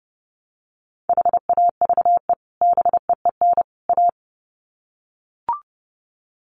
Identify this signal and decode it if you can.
Morse code, heard as a keyed tone, about 30 wpm, 700 Hz. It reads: HAVE BEEN A
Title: Sets of 3 Words 30wpm